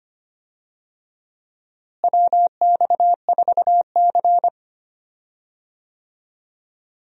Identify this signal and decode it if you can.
Morse code, heard as a keyed tone, about 25 wpm, 700 Hz. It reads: WX4C
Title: USA Call Signs - 4 Characters - Repeated 2x - 25wpm